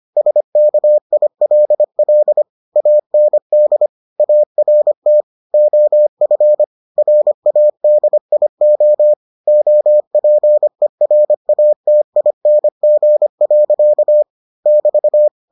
Morse code, heard as a keyed tone, about 25 wpm, 600 Hz. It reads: SKILL AND ART OF RADIO OPERATING. <BT>